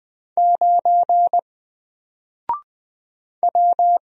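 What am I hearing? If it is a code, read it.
Morse code, heard as a keyed tone, about 20 wpm, 700 Hz. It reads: 9 W
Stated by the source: Single letter and number Morse drill at 20 wpm, from N4LCD.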